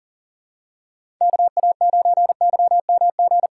Morse code, heard as a keyed tone, about 37 wpm, 700 Hz. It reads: KA9YMG
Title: K1USN SST QSO Practice - 40wpm